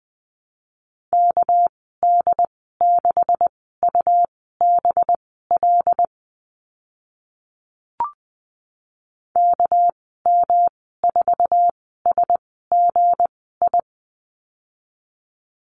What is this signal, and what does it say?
Morse code, heard as a keyed tone, about 20 wpm, spaced as for 14 wpm, 700 Hz. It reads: KD6UBL KM4SGI
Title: 245 Lesson 39g - Letter z - Call Signs 20wpm